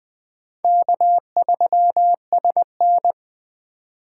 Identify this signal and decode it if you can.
Morse code, heard as a keyed tone, about 20 wpm, 700 Hz. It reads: K3SN
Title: Speed Racing - USA Call Signs 17wpm